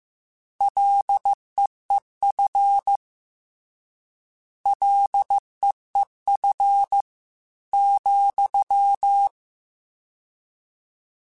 Morse code, heard as a keyed tone, about 15 wpm, 800 Hz. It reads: LEEF LEEF ,